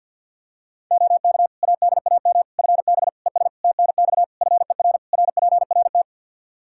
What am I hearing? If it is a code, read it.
Morse code, heard as a keyed tone, about 47 wpm, 700 Hz. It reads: OK ADAM FB ES TNX FER RPRT
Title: Practice QSO - Exchange 2 - Info - 50wpm